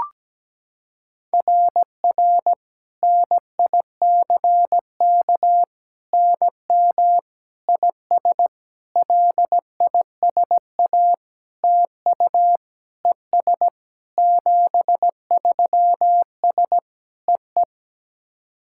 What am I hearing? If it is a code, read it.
Morse code, heard as a keyed tone, about 17 wpm, 700 Hz. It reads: RR NICK NM IS LISA TU ES 73S EE
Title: SOTA QSO Practice 17wpm